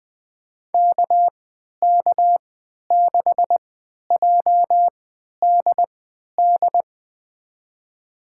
Morse code, heard as a keyed tone, about 21 wpm, spaced as for 11 wpm, 700 Hz. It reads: KK6JDD
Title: USA Call Signs - 3x Character Spacing - 20wpm